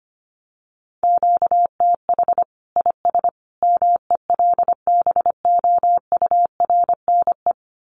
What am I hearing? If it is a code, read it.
Morse code, heard as a keyed tone, about 25 wpm, 700 Hz. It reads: QTH IS MELBOURNE
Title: Common QSO Phrases 25wpm